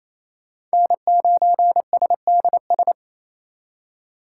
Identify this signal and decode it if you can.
Morse code, heard as a keyed tone, about 36 wpm, 700 Hz. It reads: N9SDS